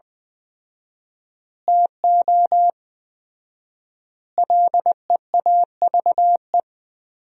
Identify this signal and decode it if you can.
Morse code, heard as a keyed tone, about 20 wpm, 700 Hz. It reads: TO LEAVE
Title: 4x Word-Spacing - Sentences from Top 400 Words - 20wpm